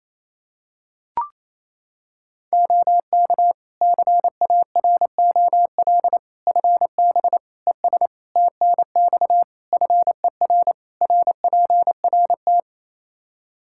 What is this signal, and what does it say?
Morse code, heard as a keyed tone, about 28 wpm, 700 Hz. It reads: OK CAROL FB ES TNX FER RPRT
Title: K1USN SST QSO Practice 28wpm